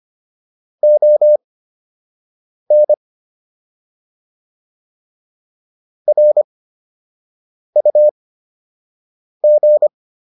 Morse code, heard as a keyed tone, about 25 wpm, spaced as for 6 wpm, 600 Hz. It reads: ON RUG